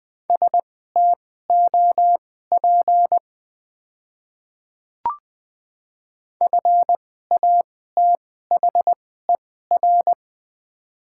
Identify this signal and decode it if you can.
Morse code, heard as a keyed tone, about 20 wpm, spaced as for 15 wpm, 700 Hz. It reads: STOP FATHER
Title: 2x Character Spacing - Top 300 Words - 20wpm